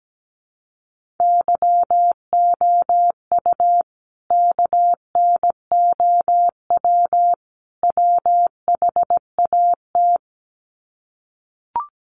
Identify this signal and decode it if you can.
Morse code, heard as a keyed tone, about 17 wpm, 700 Hz. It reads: YOU KNOW WHAT
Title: Sets of 3 Words 17wpm